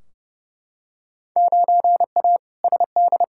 Morse code, heard as a keyed tone, about 29 wpm, 700 Hz. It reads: 9A SD
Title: Field Day Exchanges Encore - 30wpm